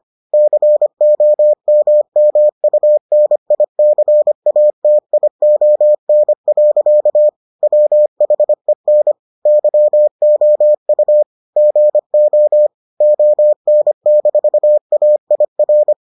Morse code, heard as a keyed tone, about 25 wpm, 600 Hz. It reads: COMMUNICATION. WHEN YOU GO ON-AIR